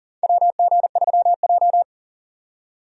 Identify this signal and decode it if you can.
Morse code, heard as a keyed tone, about 40 wpm, 700 Hz. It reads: WG3J
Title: USA Call Signs - 4 Characters - 40wpm